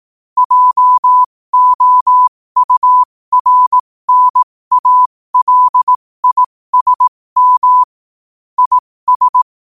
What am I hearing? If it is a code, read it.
Morse code, heard as a keyed tone, about 18 wpm, spaced as for 15 wpm, 1000 Hz. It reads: JOURNALISM IS
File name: qotdfiles / 15wpm